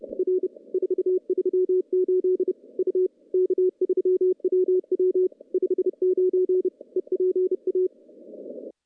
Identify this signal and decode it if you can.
Morse code, heard as a keyed tone, about 31 wpm, 380 Hz. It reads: R 438 U K3WW 59 EPA
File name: contests / CW-SS-k5tr- -1300